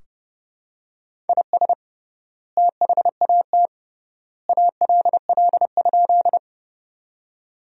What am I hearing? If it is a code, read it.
Morse code, heard as a keyed tone, about 32 wpm, 700 Hz. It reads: IS THAT ALL?
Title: 3x Word-Spacing - Sentences from Top 100 Words - 30wpm